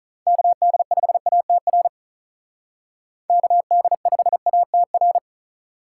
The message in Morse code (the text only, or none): KD5ATR KD5ATR